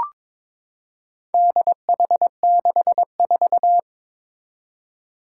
Morse code, heard as a keyed tone, about 22 wpm, 700 Hz. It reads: DH64
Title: Four Challenging Letter-Number Combinations 22wpm